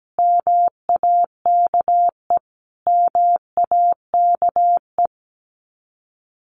Morse code, heard as a keyed tone, about 17 wpm, 700 Hz. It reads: MAKE MAKE
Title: Top 300 Words Repeated 17wpm